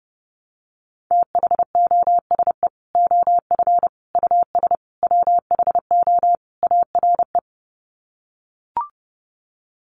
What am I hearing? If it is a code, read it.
Morse code, heard as a keyed tone, about 30 wpm, 700 Hz. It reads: THOSE OF US WHO ARE
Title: Sets of 5 Words Encore2 30wpm